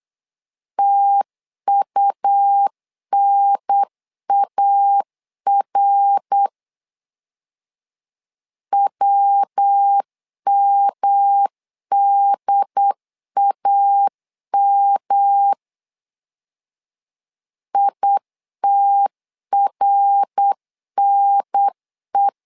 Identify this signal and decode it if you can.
Morse code, heard as a keyed tone, about 8 wpm, 800 Hz. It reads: TUNAR WMDAM ITRNE
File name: LEZIONI / 3 MWRO / 3 MWR 8wpm 2444